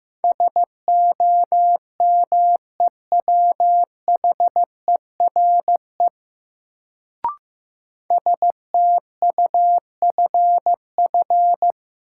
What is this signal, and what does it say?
Morse code, heard as a keyed tone, about 15 wpm, 700 Hz. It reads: SOMEWHERE STUFF
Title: Sentences from Top 1400 Words - New Word Drill Alt - 15wpm